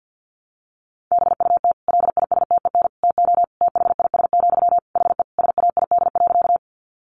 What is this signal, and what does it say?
Morse code, heard as a keyed tone, about 50 wpm, 700 Hz. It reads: BUT LISTEN TO THIS, HE SAID.
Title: Sentences from Top 400 Words 50wpm